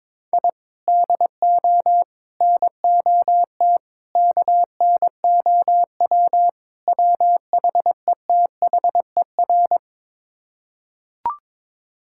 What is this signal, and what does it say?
Morse code, heard as a keyed tone, about 22 wpm, 700 Hz. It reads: I DO NOT KNOW WHETHER
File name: practice / 22wpm / Build A Phrase - 5-Word Phrases - 22wpm